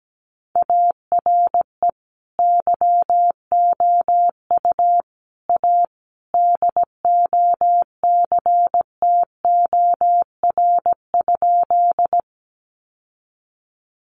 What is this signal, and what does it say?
Morse code, heard as a keyed tone, about 17 wpm, 700 Hz. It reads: ARE YOU A DOCTOR?
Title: Sentences from Top 1500 Words 17wpm